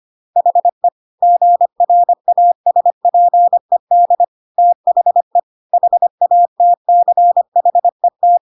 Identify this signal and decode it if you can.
Morse code, heard as a keyed tone, about 25 wpm, 700 Hz. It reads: HE GRASPED THE HATCHET
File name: Pinocchio / Pinocchio ch1 25 25 ik1hge